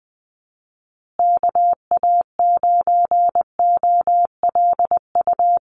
Morse code, heard as a keyed tone, about 20 wpm, 700 Hz. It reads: KA9OLU
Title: USA Call Signs Repeated 2x 20wpm